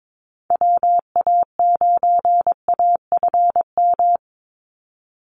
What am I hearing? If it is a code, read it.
Morse code, heard as a keyed tone, about 22 wpm, 700 Hz. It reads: WA9AFM
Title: USA Call Signs 22wpm